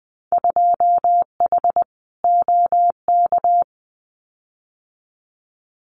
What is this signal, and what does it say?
Morse code, heard as a keyed tone, about 20 wpm, 700 Hz. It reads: 2H OK
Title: Winter Field Day Exchanges Encore 20wpm